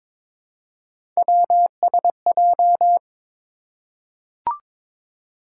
Morse code, heard as a keyed tone, about 22 wpm, 700 Hz. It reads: WSJ
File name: practice / 22wpm / Three Letter-Number Combinations - 22wpm